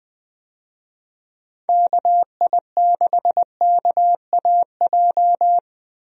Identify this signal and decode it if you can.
Morse code, heard as a keyed tone, about 20 wpm, 700 Hz. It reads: KI6KAJ
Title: USA Call Signs - 6 Characters - 20wpm